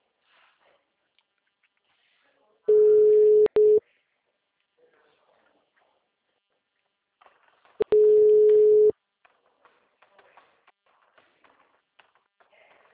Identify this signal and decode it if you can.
Ringback tone (Europe, 425 Hz).